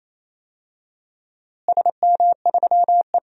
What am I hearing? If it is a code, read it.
Morse code, heard as a keyed tone, about 28 wpm, 700 Hz. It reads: SM3E
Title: DXCC-100 Call Signs - 4 Characters - Repeated 3x - 28wpm